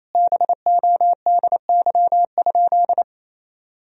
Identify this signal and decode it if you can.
Morse code, heard as a keyed tone, about 28 wpm, 700 Hz. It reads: BODY?
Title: Sentences from Top 1000 Words - Review Part 2 - 28wpm